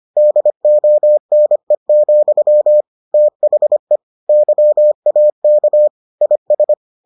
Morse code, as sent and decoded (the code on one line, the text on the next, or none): -.. --- -. --..-- / - .... / -.-- .- -.- / .. ...
DONE, THE YAK IS